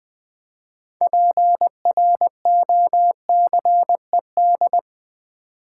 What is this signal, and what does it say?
Morse code, heard as a keyed tone, about 20 wpm, 700 Hz. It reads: PROCED
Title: Build A Word - 3 Letters At A Time - 9-Letter Words - 20wpm